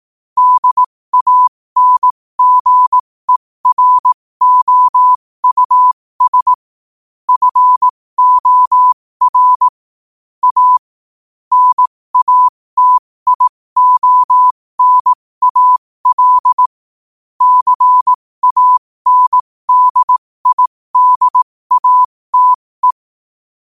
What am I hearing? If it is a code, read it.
Morse code, heard as a keyed tone, about 18 wpm, spaced as for 15 wpm, 1000 Hz. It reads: DANGEROUS FOR A NATIONAL CANDIDATE
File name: qotdfiles / 15wpm